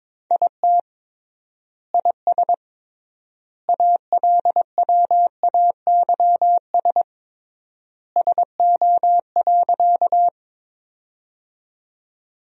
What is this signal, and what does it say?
Morse code, heard as a keyed tone, about 22 wpm, 700 Hz. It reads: IT IS ALWAYS SO.
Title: 3x Word-Spacing - Sentences from Top 300 Words - 22wpm